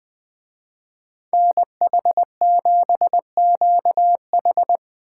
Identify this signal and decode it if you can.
Morse code, heard as a keyed tone, about 24 wpm, 700 Hz. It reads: NH7QH
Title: Speed Racing - USA 2x2 Call Signs 20wpm